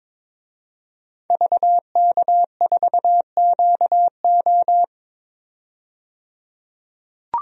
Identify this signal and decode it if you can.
Morse code, heard as a keyed tone, about 22 wpm, 700 Hz. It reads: VK4QO
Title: DXCC-100 Call Signs - 5 Characters - 22wpm